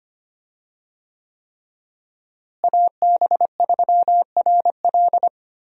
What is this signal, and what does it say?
Morse code, heard as a keyed tone, about 25 wpm, 700 Hz. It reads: AB3RL
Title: USA Call Signs - 5 Characters - Repeated 3x - 25wpm